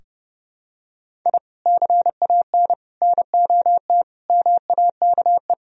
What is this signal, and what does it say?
Morse code, heard as a keyed tone, about 28 wpm, 700 Hz. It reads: I CAN NOT MAKE
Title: Sentences from Top 1000 Words - Review Part 3 - 30wpm